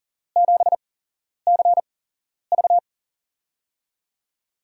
Morse code, heard as a keyed tone, about 43 wpm, spaced as for 11 wpm, 700 Hz. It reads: 7CV
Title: USA Call Signs - 5 Characters - 8x Character Spacing - 40wpm